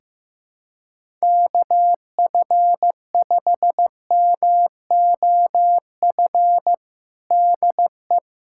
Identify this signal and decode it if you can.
Morse code, heard as a keyed tone, about 15 wpm, 700 Hz. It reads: KF5MOF DE
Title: USA Callsign DE Callsign - 6 Characters - 15wpm